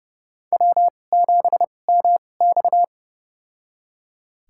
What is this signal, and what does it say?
Morse code, heard as a keyed tone, about 32 wpm, spaced as for 22 wpm, 700 Hz. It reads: W7MX
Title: USA Call Signs - 4 Characters - 2x Character Spacing - 30wpm